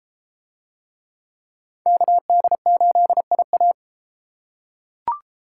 Morse code, heard as a keyed tone, about 32 wpm, 700 Hz. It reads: KD8IA